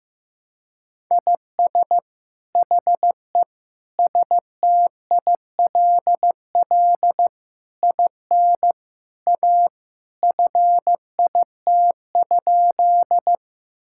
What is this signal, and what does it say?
Morse code, heard as a keyed tone, about 15 wpm, 700 Hz. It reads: IS HE STILL IN A FIT?
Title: Sentences from Top 1000 Words - Review Part 5 - 15wpm